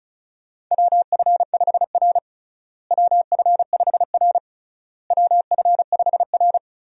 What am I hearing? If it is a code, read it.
Morse code, heard as a keyed tone, about 36 wpm, 700 Hz. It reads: WF5R WF5R WF5R
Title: USA Call Signs - 4 Characters - Repeated 3x - 35wpm